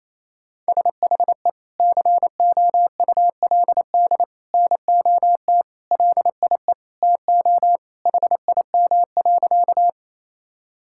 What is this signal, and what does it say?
Morse code, heard as a keyed tone, about 28 wpm, 700 Hz. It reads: SHE COULD NOT LIE TO HIM.
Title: Sentences from Top 1600 Words 28wpm